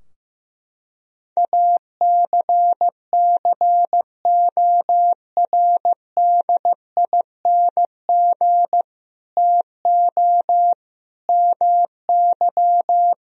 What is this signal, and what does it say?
Morse code, heard as a keyed tone, about 16 wpm, 700 Hz. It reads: ACCORDING TO MY
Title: Sentences from Top 1000 Words - Review Part 4 - 15wpm